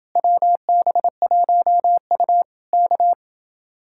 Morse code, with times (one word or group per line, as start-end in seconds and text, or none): 0.2-2.4 WB1U
2.7-3.1 K